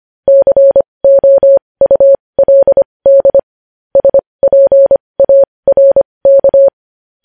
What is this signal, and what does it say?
Morse code, heard as a keyed tone, about 25 wpm, spaced as for 20 wpm, 550 Hz. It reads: COULD SPARK
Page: news-headlines-in-morse-code-at-20-wpm